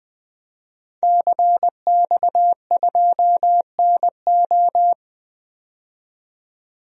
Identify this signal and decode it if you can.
Morse code, heard as a keyed tone, about 20 wpm, 700 Hz. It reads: CX2NO
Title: DXCC-100 Call Signs - 5 Characters - 20wpm